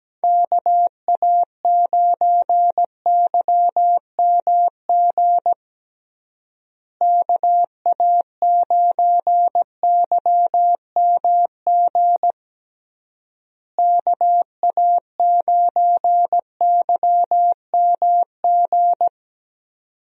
Morse code, heard as a keyed tone, about 17 wpm, 700 Hz. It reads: KA9YMG KA9YMG KA9YMG
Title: USA Call Signs - 6 Characters - Repeated 3x - 17wpm